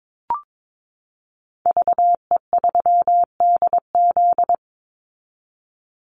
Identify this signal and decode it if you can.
Morse code, heard as a keyed tone, about 22 wpm, 700 Hz. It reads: VE3DZ
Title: Call Signs Repeated 3x - 22wpm